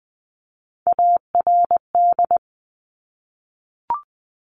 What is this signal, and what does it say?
Morse code, heard as a keyed tone, about 20 wpm, 700 Hz. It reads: ARD
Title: Top Three Letter Combinations in Words 20wpm